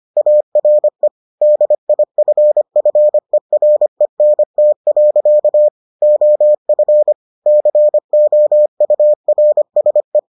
Morse code, heard as a keyed tone, about 25 wpm, 600 Hz. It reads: ARE DIFFERENT. OF COURSE